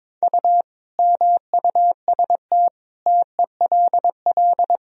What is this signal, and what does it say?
Morse code, heard as a keyed tone, about 22 wpm, 700 Hz. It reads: U MUST TELL
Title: Sentences from Top 200 Words and Abbreviations 22wpm